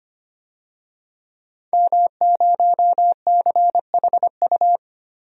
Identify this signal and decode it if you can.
Morse code, heard as a keyed tone, about 25 wpm, 700 Hz. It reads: M0CHU
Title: DXCC-100 Call Signs - 5 Characters - Repeated 2x - 25wpm